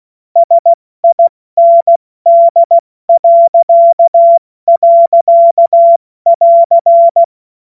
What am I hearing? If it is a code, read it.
Morse code, heard as a keyed tone, about 16 wpm, spaced as for 14 wpm, 680 Hz. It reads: SIND..<AR>
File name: Morsekurs / Witze / Witz 14.November 2023 70 F80